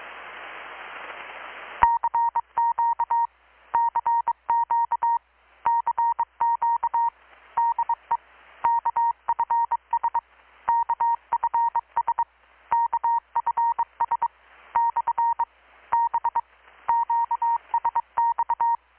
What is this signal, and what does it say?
Morse code, heard as a keyed tone, about 22 wpm, 930 Hz. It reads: CQ CQ CQ DE KFS KFS KFS / B QSX
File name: blog / swl / 6-abril-2019-12695-USB-KPS